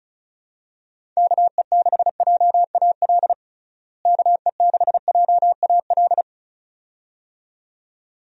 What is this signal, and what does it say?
Morse code, heard as a keyed tone, about 34 wpm, 700 Hz. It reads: KE6JAL KE6JAL
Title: USA Call Signs - 6 Characters - Repeated 2x - 35wpm